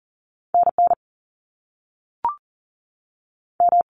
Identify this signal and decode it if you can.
Morse code, heard as a keyed tone, about 40 wpm, 700 Hz. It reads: NN M
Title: Top Two Letter Combinations in Words 40wpm